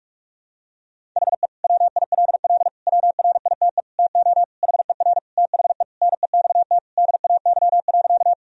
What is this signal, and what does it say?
Morse code, heard as a keyed tone, about 43 wpm, 700 Hz. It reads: HE WILL WRITE TO HER THE NEXT DAY.
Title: Sentences from Top 1000 Words - Review Part 1 - 45wpm